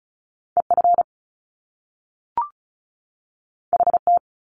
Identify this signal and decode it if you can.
Morse code, heard as a keyed tone, about 36 wpm, 700 Hz. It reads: EF HT
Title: Top Two Letter Combinations in Words 35wpm